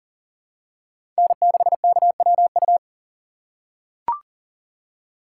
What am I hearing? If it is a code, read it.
Morse code, heard as a keyed tone, about 40 wpm, 700 Hz. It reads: N6KWU
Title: USA Call Signs - 5 Characters - 6x Character Spacing - 40wpm